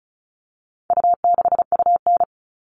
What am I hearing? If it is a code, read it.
Morse code, heard as a keyed tone, about 34 wpm, 700 Hz. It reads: U6UN
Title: All Cut Numbers - 4-Digit 35wpm